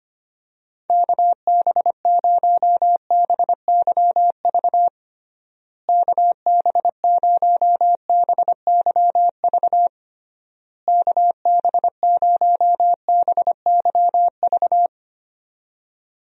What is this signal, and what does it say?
Morse code, heard as a keyed tone, about 25 wpm, 700 Hz. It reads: KB0BYV KB0BYV KB0BYV
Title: USA Call Signs - 6 Characters - Repeated 3x - 25wpm